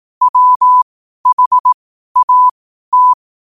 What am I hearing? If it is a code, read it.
Morse code, heard as a keyed tone, about 18 wpm, spaced as for 12 wpm, 1000 Hz. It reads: WHAT
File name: qotdfiles / 12wpm